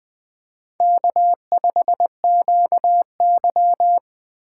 Morse code, heard as a keyed tone, about 20 wpm, 700 Hz. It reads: K5QY
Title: USA Call Signs - 4 Characters - 3x Character Spacing - 20wpm